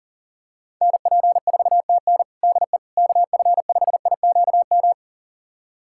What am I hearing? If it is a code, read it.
Morse code, heard as a keyed tone, about 37 wpm, 700 Hz. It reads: NP4TN DE KF5IQM
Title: USA Callsign DE Callsign 40wpm